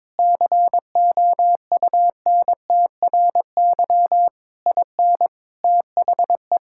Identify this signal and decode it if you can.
Morse code, heard as a keyed tone, about 21 wpm, 700 Hz. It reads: COUNTRY IN THE
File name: practice / 22wpm / Sentences with 8-Word Phrases - 22wpm